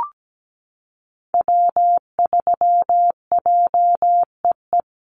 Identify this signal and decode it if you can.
Morse code, heard as a keyed tone, about 17 wpm, 700 Hz. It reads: W3JEE
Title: USA Call Signs Repeated 2x 17wpm